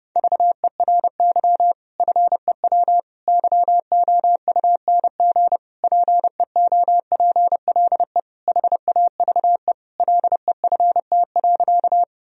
Morse code, heard as a keyed tone, about 32 wpm, 700 Hz. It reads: VERY FEW YOUNG PEOPLE HAVE LEFT.
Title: Sentences from Top 1000 Words - Review Part 1 - 30wpm